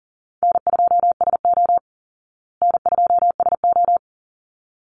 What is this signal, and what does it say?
Morse code, heard as a keyed tone, about 40 wpm, 700 Hz. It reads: N2SO N2SO
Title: Easy Call Signs Repeated 2x - 40wpm